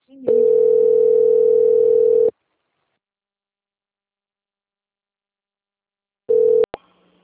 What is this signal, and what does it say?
Ringback tone (North America), 440 + 480 Hz.
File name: out- - - - - .4335